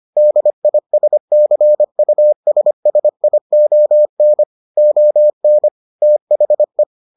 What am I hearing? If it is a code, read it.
Morse code, heard as a keyed tone, about 25 wpm, 600 Hz. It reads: DISCUSSION ON THE